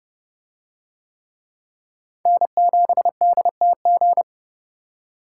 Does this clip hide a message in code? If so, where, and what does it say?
Yes: Morse code from 2.3 to 4.2 s, heard as a keyed tone, about 30 wpm, 700 Hz. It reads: N7DTG